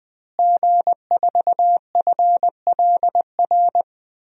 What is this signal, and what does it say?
Morse code, heard as a keyed tone, about 20 wpm, 700 Hz. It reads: G4FLR
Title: DXCC-100 Call Signs - 5 Characters - Repeated 3x - 20wpm